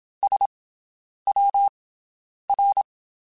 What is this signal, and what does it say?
Morse code, heard as a keyed tone, about 27 wpm, spaced as for 9 wpm, 800 Hz. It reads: SWR
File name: QSOs Spaced